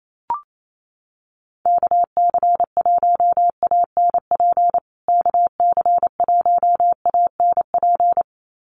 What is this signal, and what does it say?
Morse code, heard as a keyed tone, about 28 wpm, 700 Hz. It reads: KC1ANP KC1ANP
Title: USA Call Signs Repeated 2x 28wpm